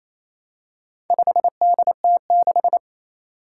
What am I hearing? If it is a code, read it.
Morse code, heard as a keyed tone, about 28 wpm, 700 Hz. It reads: 5DT6
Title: Four Challenging Letter-Number Combinations 28wpm